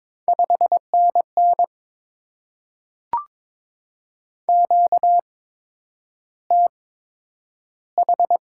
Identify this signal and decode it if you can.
Morse code, heard as a keyed tone, about 22 wpm, spaced as for 9 wpm, 700 Hz. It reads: 5NN QTH
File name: Top 20 QSO Elements - 8x Character Spacing - 22wpm